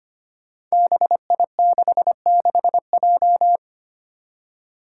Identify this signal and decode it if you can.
Morse code, heard as a keyed tone, about 25 wpm, 700 Hz. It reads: BI66J
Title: Five Challenging Letter-Number Combinations 25wpm